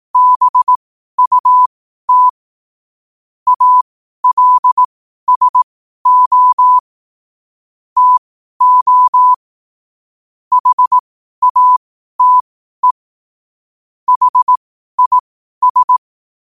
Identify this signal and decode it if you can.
Morse code, heard as a keyed tone, about 18 wpm, spaced as for 12 wpm, 1000 Hz. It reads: BUT ALSO TO HATE HIS
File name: qotdfiles / 12wpm